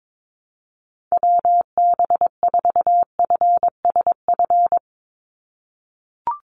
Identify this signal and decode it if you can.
Morse code, heard as a keyed tone, about 22 wpm, 700 Hz. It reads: WB4FSF
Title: Call Signs - 22wpm